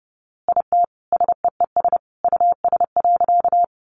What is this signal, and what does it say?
Morse code, heard as a keyed tone, about 30 wpm, 700 Hz. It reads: IT SEES US.
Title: A Sound Of Thunder 30wpm